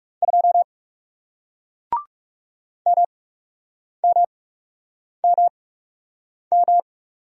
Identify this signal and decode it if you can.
Morse code, heard as a keyed tone, about 39 wpm, spaced as for 8 wpm, 700 Hz. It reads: 2 MMMM